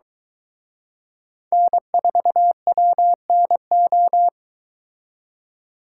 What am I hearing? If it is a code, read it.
Morse code, heard as a keyed tone, about 22 wpm, 700 Hz. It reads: N4WNO